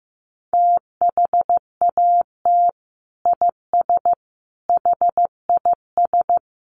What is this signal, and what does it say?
Morse code, heard as a keyed tone, about 14 wpm, 700 Hz. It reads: THAT IS HIS
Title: Sentences from Top 1200 Words 15wpm